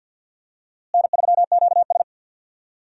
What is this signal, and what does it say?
Morse code, heard as a keyed tone, about 52 wpm, 700 Hz. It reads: N3QS
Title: USA Call Signs - 4 Characters - Repeated 3x - 50wpm